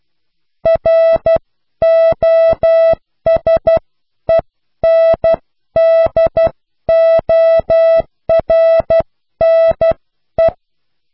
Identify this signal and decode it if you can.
Morse code, heard as a keyed tone, about 12 wpm, spaced as for 10 wpm, 660 Hz. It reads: ROSENDORNE
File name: cw / pseudo worden(10)